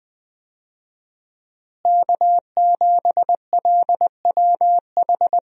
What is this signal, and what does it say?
Morse code, heard as a keyed tone, about 20 wpm, 700 Hz. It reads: K7LWH
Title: USA Call Signs - 5 Characters - Repeated 3x - 20wpm